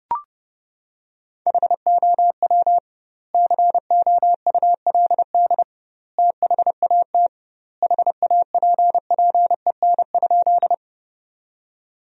Morse code, heard as a keyed tone, about 30 wpm, 700 Hz. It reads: HOW COULD THAT HAPPEN?
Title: 2x Word-Spacing - Sentences from Top 400 Words - 30wpm